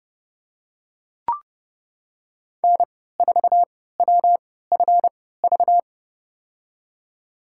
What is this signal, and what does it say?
Morse code, heard as a keyed tone, about 30 wpm, spaced as for 17 wpm, 700 Hz. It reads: N4WFV
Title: USA Call Signs - 3x Character Spacing - 30wpm